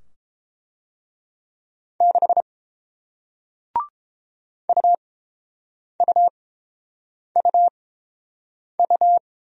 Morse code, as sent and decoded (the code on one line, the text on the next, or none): -.... / ..- ..- ..- ..-
6 UUUU